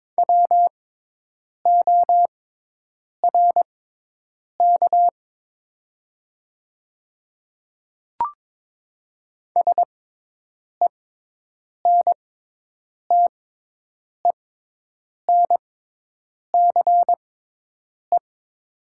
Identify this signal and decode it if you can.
Morse code, heard as a keyed tone, about 22 wpm, spaced as for 8 wpm, 700 Hz. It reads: WORK SENTENCE
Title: 6x Character Spacing - Top 300 Words - 22wpm